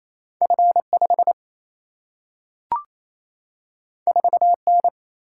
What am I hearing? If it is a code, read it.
Morse code, heard as a keyed tone, about 28 wpm, 700 Hz. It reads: F 5 4 N
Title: Single-Letter Single-Number 28wpm